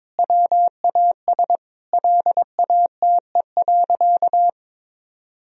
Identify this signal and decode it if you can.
Morse code, heard as a keyed tone, about 22 wpm, 700 Hz. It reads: WAS LATE.
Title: Sentences from Top 1000 Words - Review Part 3 - 22wpm